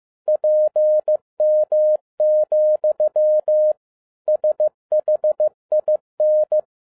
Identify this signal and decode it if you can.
Morse code, heard as a keyed tone, about 15 wpm, 610 Hz. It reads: PM, SHIN
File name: 2011-07-mp3 / n47 JOHN K3TN 15wpm